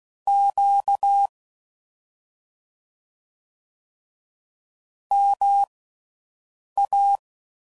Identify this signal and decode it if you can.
Morse code, heard as a keyed tone, about 16 wpm, spaced as for 6 wpm, 800 Hz. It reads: Q MA
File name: Mixed / CW Comparisons / Comparisons Broken Characters